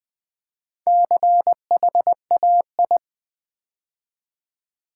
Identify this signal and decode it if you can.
Morse code, heard as a keyed tone, about 20 wpm, 700 Hz. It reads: CHAI
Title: Build A Word - 2 Letters At A Time - 8-Letter Words - 20wpm